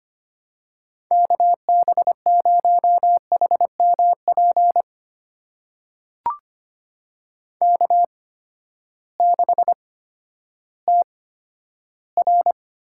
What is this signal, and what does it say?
Morse code, heard as a keyed tone, about 25 wpm, spaced as for 12 wpm, 700 Hz. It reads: KB0HMP K6TR
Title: USA Call Signs - 8x Character Spacing - 25wpm